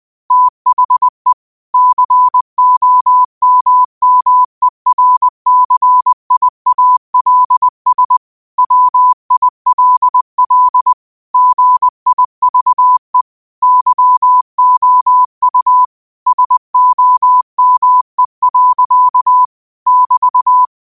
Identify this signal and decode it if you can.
Morse code, heard as a keyed tone, about 20 wpm, 1000 Hz. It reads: THE COMMERCIALS WILL GIVE YOU SOME. <BT>